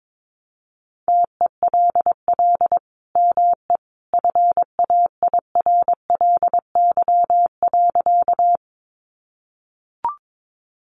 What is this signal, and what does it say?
Morse code, heard as a keyed tone, about 22 wpm, 700 Hz. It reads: TELL ME FAIRLY.
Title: Sentences from Top 1100 Words 22wpm